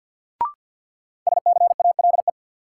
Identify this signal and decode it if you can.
Morse code, heard as a keyed tone, about 51 wpm, 700 Hz. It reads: SCALE
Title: Sentences from Top 1000 Words - Review Part 4 - New Word Drill - 50wpm